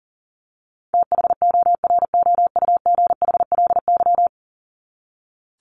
Morse code, heard as a keyed tone, about 37 wpm, 700 Hz. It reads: THOROUGHLY
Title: LY Suffix - Top 100 Words with Base Words 40wpm